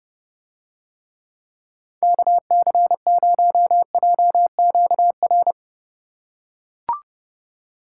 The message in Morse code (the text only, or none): KC0JQR